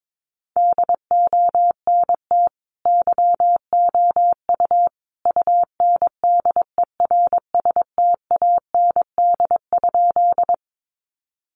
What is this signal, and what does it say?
Morse code, heard as a keyed tone, about 22 wpm, 700 Hz. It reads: DONT YOU UNDERSTAND?
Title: Sentences from Top 600 Words 22wpm